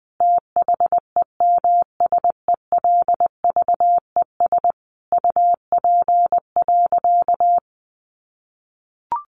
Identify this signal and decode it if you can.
Morse code, heard as a keyed tone, about 20 wpm, 700 Hz. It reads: THEMSELVES UP.
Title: Sentences from Top 1400 Words 20wpm